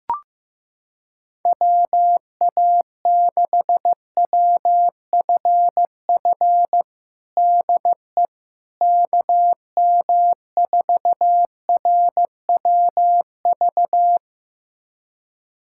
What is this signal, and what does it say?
Morse code, heard as a keyed tone, about 15 wpm, 700 Hz. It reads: WA6WFF DE KM4RWV
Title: USA Callsign DE Callsign 15wpm